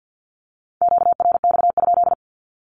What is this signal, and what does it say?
Morse code, heard as a keyed tone, about 48 wpm, 700 Hz. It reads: QRX?
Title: Q-Codes 50wpm